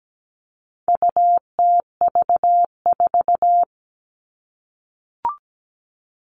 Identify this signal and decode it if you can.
Morse code, heard as a keyed tone, about 17 wpm, 700 Hz. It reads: UTV4
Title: All Cut Numbers - 4-Digit 17wpm